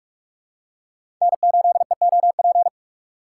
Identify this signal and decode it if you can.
Morse code, heard as a keyed tone, about 46 wpm, 700 Hz. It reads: N8EOP
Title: USA Call Signs - 5 Characters - 6x Character Spacing - 45wpm